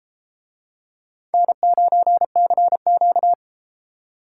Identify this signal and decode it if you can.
Morse code, heard as a keyed tone, about 31 wpm, 700 Hz. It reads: N9CQ